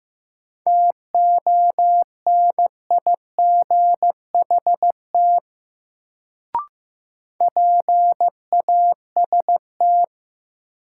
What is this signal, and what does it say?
Morse code, heard as a keyed tone, about 16 wpm, 700 Hz. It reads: TONIGHT PAST
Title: Sentences from Top 600 Words - New Word Drill - 15wpm